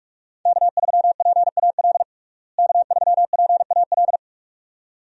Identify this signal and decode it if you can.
Morse code, heard as a keyed tone, about 42 wpm, 700 Hz. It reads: K3PAL K3PAL
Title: USA Call Signs - 5 Characters - Repeated 2x - 45wpm